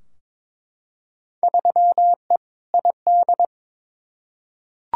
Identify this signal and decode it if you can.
Morse code, heard as a keyed tone, about 22 wpm, 700 Hz. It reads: 3E ID
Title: Field Day Exchanges Encore - 22wpm